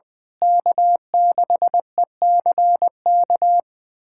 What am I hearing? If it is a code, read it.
Morse code, heard as a keyed tone, about 20 wpm, 700 Hz. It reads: K6ECK
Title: USA Callsign DE Callsign - 5 Characters - 20wpm